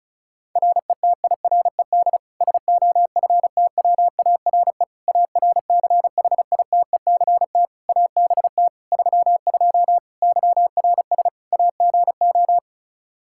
Morse code, heard as a keyed tone, about 34 wpm, 700 Hz. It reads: RETIRED SOFTWARE ARCHITECT ABT 32 YRS AGO
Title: Practice QSO - Exchange 3 - More Info - 35wpm